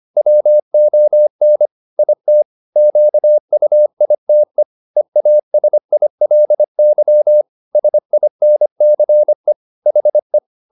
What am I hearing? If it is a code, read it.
Morse code, heard as a keyed tone, about 25 wpm, 600 Hz. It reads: WON IT QUITE EASILY SINCE HE